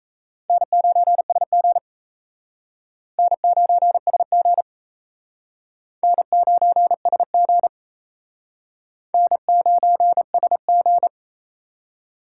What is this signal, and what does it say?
Morse code, heard as a keyed tone, about 37 wpm, 700 Hz. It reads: N9SG N9SG N9SG N9SG